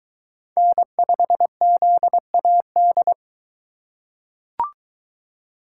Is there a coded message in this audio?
Yes: Morse code from 0.6 to 3.1 s, heard as a keyed tone, about 21 wpm, 700 Hz. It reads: N5ZAD